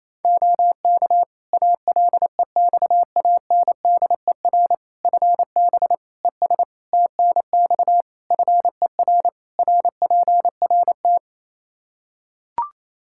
Morse code, heard as a keyed tone, about 28 wpm, 700 Hz. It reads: OK ALEXANDER FB ES TNX FER RPRT
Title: K1USN SST QSO Practice 28wpm